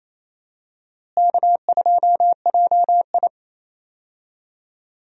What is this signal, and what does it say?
Morse code, heard as a keyed tone, about 28 wpm, 700 Hz. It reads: K2JI
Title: USA Call Signs - 4 Characters - Repeated 3x - 28wpm